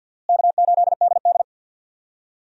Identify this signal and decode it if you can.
Morse code, heard as a keyed tone, about 51 wpm, 700 Hz. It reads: K8DD
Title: USA Call Signs - 4 Characters - Repeated 3x - 50wpm